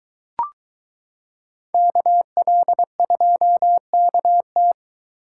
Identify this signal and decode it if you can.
Morse code, heard as a keyed tone, about 23 wpm, 700 Hz. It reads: KL2KT